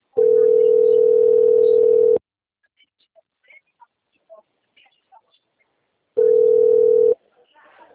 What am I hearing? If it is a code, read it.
Ringback tone (North America), 440 + 480 Hz.